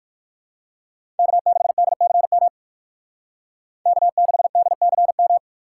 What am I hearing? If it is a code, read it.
Morse code, heard as a keyed tone, about 49 wpm, 700 Hz. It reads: K6DCM K6DCM
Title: Speed Racing - USA 1x3 Call Signs 45wpm